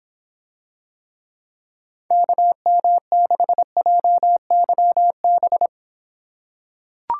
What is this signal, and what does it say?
Morse code, heard as a keyed tone, about 26 wpm, 700 Hz. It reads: KM6JYB